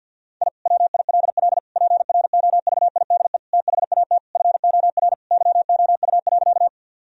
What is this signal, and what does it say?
Morse code, heard as a keyed tone, about 50 wpm, 700 Hz. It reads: I WILL PROVIDE THAT FOR YOU.
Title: Sentences from Top 1000 Words - Review Part 3 - 50wpm